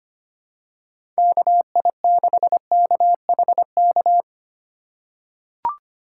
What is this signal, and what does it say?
Morse code, heard as a keyed tone, about 25 wpm, 700 Hz. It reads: KI6KHK
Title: USA Call Signs - 6 Characters - 2x Character Spacing - 25wpm